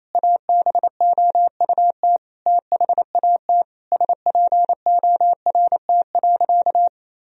Morse code, heard as a keyed tone, about 28 wpm, 700 Hz. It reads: ABOUT THAT SPORT.
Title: Sentences from Top 1000 Words - Review Part 4 - 28wpm